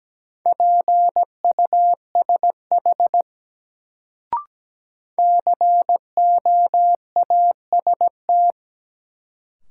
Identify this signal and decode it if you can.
Morse code, heard as a keyed tone, about 17 wpm, 700 Hz. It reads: PUSH COAST
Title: Sentences from Top 1300 Words - New Word Drill Alt - 17wpm